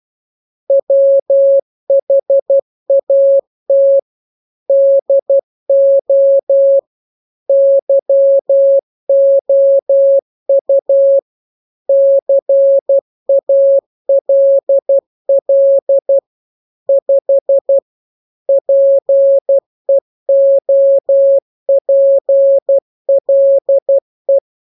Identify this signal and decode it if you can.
Morse code, heard as a keyed tone, about 12 wpm, 550 Hz. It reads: WHAT DO YOU CALL 5 PEOPLE